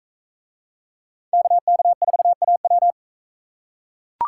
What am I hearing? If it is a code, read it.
Morse code, heard as a keyed tone, about 41 wpm, 700 Hz. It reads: KK4AW